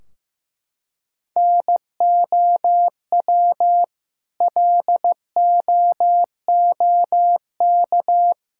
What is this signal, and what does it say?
Morse code, heard as a keyed tone, about 15 wpm, 700 Hz. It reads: NOW LOOK
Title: Sentences from Top 1600 Words 15wpm